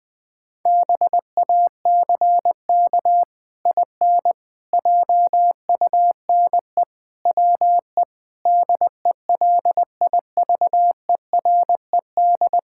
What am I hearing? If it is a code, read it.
Morse code, heard as a keyed tone, about 19 wpm, 700 Hz. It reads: BACK IN JUNE WE DELIVERED
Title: Pangram Challenge v2 - 20wpm